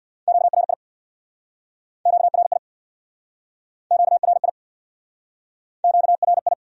Morse code, heard as a keyed tone, about 63 wpm, 700 Hz. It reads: QRI QRI QRI QRI